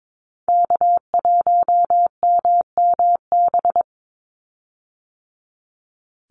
Morse code, heard as a keyed tone, about 22 wpm, 700 Hz. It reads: K1MMB
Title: Call Signs - 22wpm